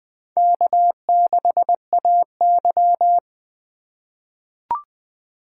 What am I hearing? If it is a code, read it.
Morse code, heard as a keyed tone, about 20 wpm, 700 Hz. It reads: K6AY
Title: USA Call Signs - 4 Characters - 2x Character Spacing - 20wpm